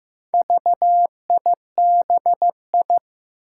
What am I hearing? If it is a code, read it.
Morse code, heard as a keyed tone, about 15 wpm, 700 Hz. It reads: VIBI
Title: Four Challenging Letter-Number Combinations 15wpm